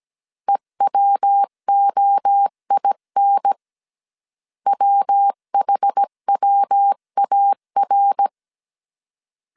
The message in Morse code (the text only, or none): EWOIN WHWAR